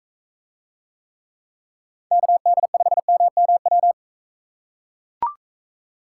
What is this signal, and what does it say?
Morse code, heard as a keyed tone, about 42 wpm, 700 Hz. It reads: KD5MMW